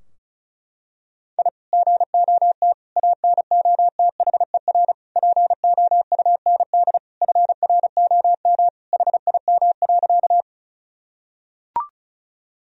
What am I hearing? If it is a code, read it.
Morse code, heard as a keyed tone, about 36 wpm, 700 Hz. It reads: I GOT ANOTHER POUND FROM HIM.
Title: Sentences from Top 1000 Words - Review Part 2 - 35wpm